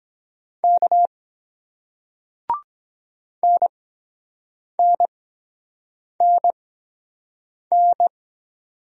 Morse code, heard as a keyed tone, about 24 wpm, spaced as for 7 wpm, 700 Hz. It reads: K NNNN